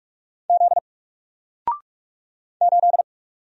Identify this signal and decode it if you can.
Morse code, heard as a keyed tone, about 45 wpm, 700 Hz. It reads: Z 8
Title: Single Letter-Number 45wpm